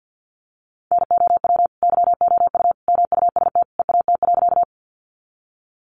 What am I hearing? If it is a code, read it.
Morse code, heard as a keyed tone, about 50 wpm, 700 Hz. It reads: NOW YOU MUST EAT.
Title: Sentences from Top 300 Words 50wpm